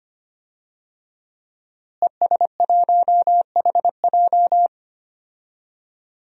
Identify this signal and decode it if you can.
Morse code, heard as a keyed tone, about 25 wpm, 700 Hz. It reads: ES1HJ
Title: DXCC-100 Call Signs - 5 Characters - Repeated 3x - 25wpm